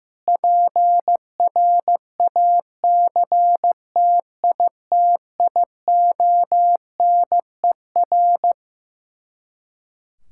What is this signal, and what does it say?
Morse code, heard as a keyed tone, about 15 wpm, 700 Hz. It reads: PRACTITIONER
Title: Practice QSO - Exchange 3 - More Info - 15wpm